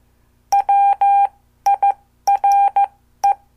Morse code, heard as a keyed tone, about 14 wpm, spaced as for 12 wpm, 800 Hz. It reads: WIRE
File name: morse / qso